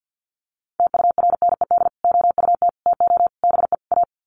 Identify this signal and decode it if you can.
Morse code, heard as a keyed tone, about 50 wpm, 700 Hz. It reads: TURNED OUT TO BE A
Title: Sets of 5 Words 50wpm